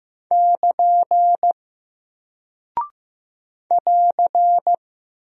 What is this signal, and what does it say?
Morse code, heard as a keyed tone, about 15 wpm, 700 Hz. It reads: <KN> <AR>
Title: Single Character (Letter:Number:Punctuation:Prosign) Alt 15wpm